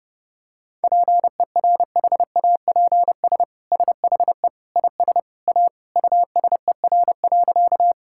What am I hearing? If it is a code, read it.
Morse code, heard as a keyed tone, about 32 wpm, 700 Hz. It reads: PERHAPS SHE IS A USER.
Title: Sentences from Top 1000 Words - Review Part 5 - 30wpm